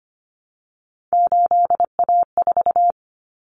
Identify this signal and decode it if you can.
Morse code, heard as a keyed tone, about 25 wpm, 700 Hz. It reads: 8A4
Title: Common Cut Numbers - 3-Digit 25wpm